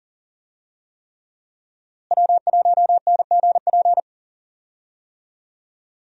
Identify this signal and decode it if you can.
Morse code, heard as a keyed tone, about 40 wpm, 700 Hz. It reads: W1NGP
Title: USA Call Signs - 5 Characters - Repeated 3x - 40wpm